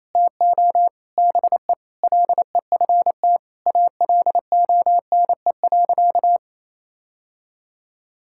Morse code, heard as a keyed tone, about 28 wpm, 700 Hz. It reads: TO BE LEFT ALONE.
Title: Sentences from Top 1000 Words - Review Part 6 - 28wpm